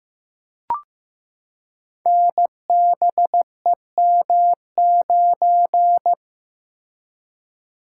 Morse code, heard as a keyed tone, about 15 wpm, 700 Hz. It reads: NBEM9
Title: Code Groups - 5 Letter:Numbers - 15wpm